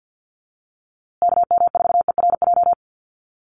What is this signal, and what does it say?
Morse code, heard as a keyed tone, about 48 wpm, 700 Hz. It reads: KM4ERJ
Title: USA Call Signs Repeated 2x 50wpm